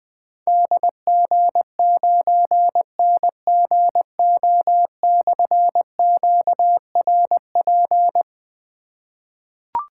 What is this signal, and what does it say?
Morse code, heard as a keyed tone, about 20 wpm, 700 Hz. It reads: DG9NGO/QRP
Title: Difficult Call Signs 20wpm